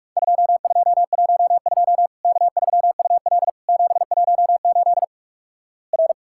Morse code, heard as a keyed tone, about 46 wpm, 700 Hz. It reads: 2212 K3UL 718 R
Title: Worked All Europe DX Contest - QTC Exchange - 45wpm